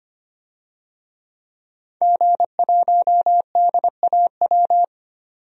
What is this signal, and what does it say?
Morse code, heard as a keyed tone, about 25 wpm, 700 Hz. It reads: G1DAW
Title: DXCC-100 Call Signs - 5 Characters - 25wpm